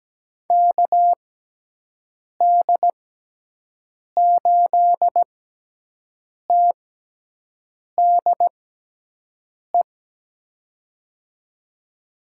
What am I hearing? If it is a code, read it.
Morse code, heard as a keyed tone, about 17 wpm, spaced as for 6 wpm, 700 Hz. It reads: KD8TDE
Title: USA Call Signs - 6 Characters - 6x Character Spacing - 17wpm